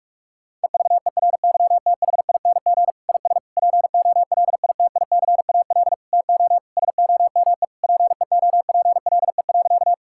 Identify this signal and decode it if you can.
Morse code, heard as a keyed tone, about 46 wpm, 700 Hz. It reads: EVERYTHING IS POLITICAL TO SOME PEOPLE.
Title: Sentences from Top 1000 Words - Review Part 5 - 45wpm